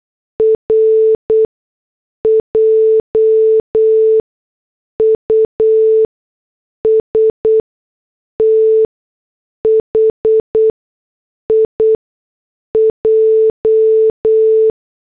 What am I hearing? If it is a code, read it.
Morse code, heard as a keyed tone, about 8 wpm, spaced as for 6 wpm, 440 Hz. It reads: R JUST HIJ